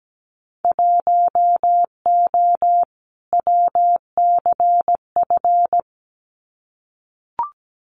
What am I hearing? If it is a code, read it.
Morse code, heard as a keyed tone, about 17 wpm, 690 Hz. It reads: 1O WCF
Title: Winter Field Day Exchanges 17wpm